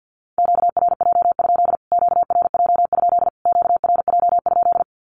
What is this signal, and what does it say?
Morse code, heard as a keyed tone, about 50 wpm, 700 Hz. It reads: QRJ? QRJ? QRJ?
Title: Q-Codes Repeated 3x 50wpm